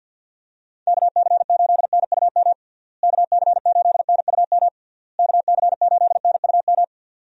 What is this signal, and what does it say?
Morse code, heard as a keyed tone, about 51 wpm, 700 Hz. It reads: KC8NUM KC8NUM KC8NUM
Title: USA Call Signs - 6 Characters - Repeated 3x - 50wpm